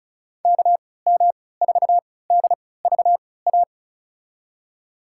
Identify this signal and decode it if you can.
Morse code, heard as a keyed tone, about 36 wpm, spaced as for 20 wpm, 700 Hz. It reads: KM4DVA
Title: USA Call Signs - 6 Characters - 3x Character Spacing - 35wpm